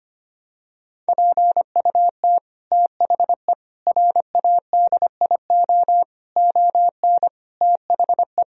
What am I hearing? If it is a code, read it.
Morse code, heard as a keyed tone, about 25 wpm, 700 Hz. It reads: PUT THE RADIO ON THE
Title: Sentences from Top 1000 Words - Review Part 3 - 25wpm